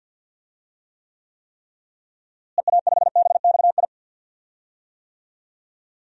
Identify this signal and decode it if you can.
Morse code, heard as a keyed tone, about 47 wpm, 700 Hz. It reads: EA5BXI
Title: DXCC-100 Call Signs - 6 Characters - Repeated 3x - 50wpm